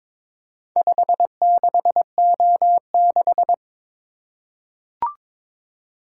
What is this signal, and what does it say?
Morse code, heard as a keyed tone, about 22 wpm, 700 Hz. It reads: 56O6
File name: practice / 22wpm / Four Letter-Number Combinations - 22wpm